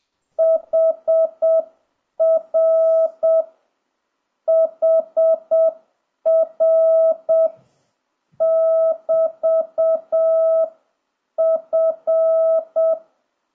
Morse code, heard as a keyed tone, about 7 wpm, 1300 Hz. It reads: HR HR <BT> F
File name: jh8cbh / 2026cw